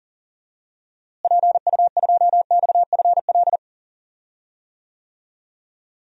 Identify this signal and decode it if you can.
Morse code, heard as a keyed tone, about 38 wpm, 700 Hz. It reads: PU2XFL
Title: DXCC-100 Call Signs - 6 Characters - Repeated 2x - 40wpm